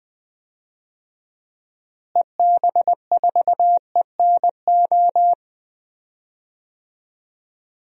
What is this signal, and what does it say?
Morse code, heard as a keyed tone, about 20 wpm, 700 Hz. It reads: EB4ENO
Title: DXCC-100 Call Signs - 6 Characters - Repeated 2x - 20wpm